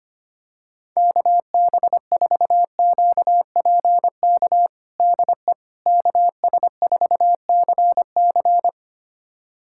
Morse code, heard as a keyed tone, about 25 wpm, 700 Hz. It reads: KB4QPK DE KS4CC
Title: USA Callsign DE Callsign 25wpm